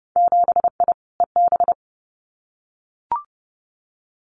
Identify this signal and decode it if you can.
Morse code, heard as a keyed tone, about 30 wpm, 700 Hz. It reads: 7I EB
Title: Winter Field Day Exchanges Encore 30wpm